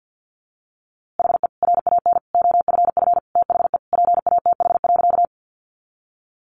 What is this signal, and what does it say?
Morse code, heard as a keyed tone, about 50 wpm, 700 Hz. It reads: HE RAN OFF THE PATH.
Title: A Sound Of Thunder 50wpm